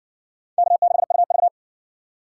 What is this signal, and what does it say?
Morse code, heard as a keyed tone, about 59 wpm, 700 Hz. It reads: K6UV